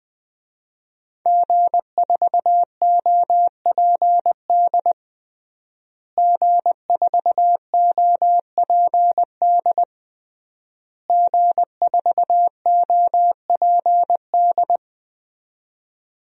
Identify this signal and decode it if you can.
Morse code, heard as a keyed tone, about 20 wpm, 700 Hz. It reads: G4OPD G4OPD G4OPD
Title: DXCC-100 Call Signs - 5 Characters - Repeated 3x - 20wpm